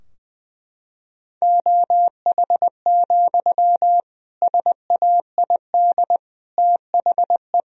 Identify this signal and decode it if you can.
Morse code, heard as a keyed tone, about 21 wpm, 700 Hz. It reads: OH, SAID THE
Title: Sentences from Top 1000 Words - Review Part 2 - 20wpm